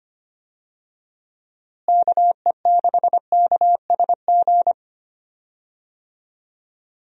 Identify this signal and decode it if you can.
Morse code, heard as a keyed tone, about 25 wpm, 700 Hz. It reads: KE6KSG
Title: USA Call Signs - 6 Characters - Repeated 2x - 25wpm